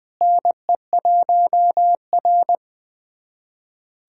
Morse code, heard as a keyed tone, about 21 wpm, 700 Hz. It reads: NE1R
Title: USA Callsign DE Callsign - 4 Characters - 20wpm